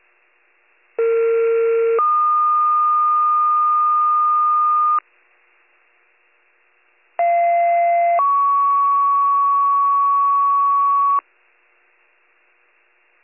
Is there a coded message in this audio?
Yes: two-tone page from 1.0 to 11.2 s, heard as a one-second tone, then a three-second tone, about 460 then 1220 Hz, about 690 then 1120 Hz.